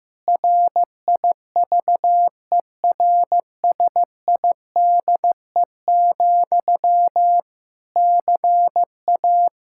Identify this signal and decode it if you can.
Morse code, heard as a keyed tone, about 15 wpm, 700 Hz. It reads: RIVERSIDE, CA
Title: Practice QSO - Exchange 1 - Beginning - 15wpm